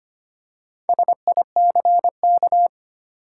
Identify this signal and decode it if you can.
Morse code, heard as a keyed tone, about 25 wpm, 700 Hz. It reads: SICK
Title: Four Letter Words - 25wpm